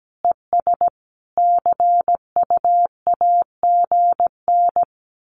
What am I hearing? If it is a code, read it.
Morse code, heard as a keyed tone, about 17 wpm, 700 Hz. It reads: ES CU AGN
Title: Common QSO Phrases 17wpm